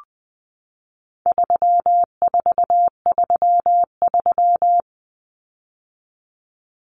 Morse code, heard as a keyed tone, about 20 wpm, 700 Hz. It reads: 3433
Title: Numbers - 4-Digit 20wpm